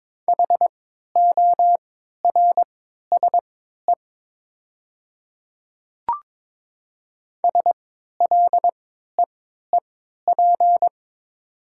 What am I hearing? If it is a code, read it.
Morse code, heard as a keyed tone, about 22 wpm, spaced as for 13 wpm, 700 Hz. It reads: HORSE SLEEP
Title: 3x Character Spacing - Top 500 Words - 22wpm